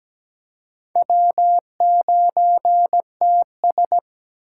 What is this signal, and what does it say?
Morse code, heard as a keyed tone, about 20 wpm, 700 Hz. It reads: W9TS
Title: Speed Racing - USA Call Signs 17wpm